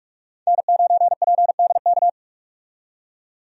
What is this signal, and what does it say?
Morse code, heard as a keyed tone, about 44 wpm, 700 Hz. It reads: N9PDK
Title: USA Call Signs - 5 Characters - 45wpm